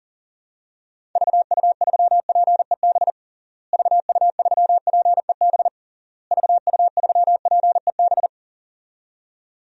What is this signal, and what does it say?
Morse code, heard as a keyed tone, about 40 wpm, 700 Hz. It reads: VU3PEB VU3PEB VU3PEB
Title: DXCC-100 Call Signs - 6 Characters - Repeated 3x - 40wpm